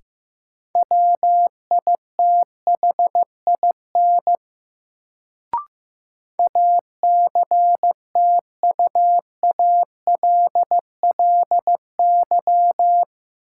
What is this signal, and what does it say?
Morse code, heard as a keyed tone, about 15 wpm, 700 Hz. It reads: WITHIN ACTUALLY
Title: Sentences from Top 700 Words - New Word Drill - 15wpm